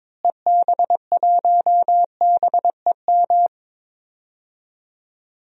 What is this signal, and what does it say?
Morse code, heard as a keyed tone, about 21 wpm, 700 Hz. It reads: EB1BEM
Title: DXCC-100 Call Signs - 6 Characters - Repeated 3x - 22wpm